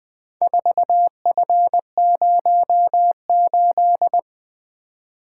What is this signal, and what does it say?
Morse code, heard as a keyed tone, about 23 wpm, 700 Hz. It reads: 4F08
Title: Speed-Racing - Four Letter-Number Combinations - 20wpm